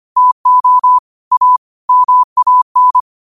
Morse code, heard as a keyed tone, about 25 wpm, 1000 Hz. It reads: TO A MAN
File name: qotdfiles / 25wpm